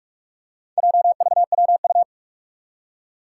Morse code, heard as a keyed tone, about 46 wpm, 700 Hz. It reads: JVWU
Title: Four Challenging Letter-Number Combinations 45wpm